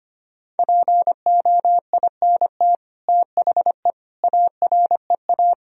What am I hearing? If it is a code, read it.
Morse code, heard as a keyed tone, about 25 wpm, 700 Hz. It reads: POINT THE AREA
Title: Sentences from Top 1000 Words - Review Part 2 - 25wpm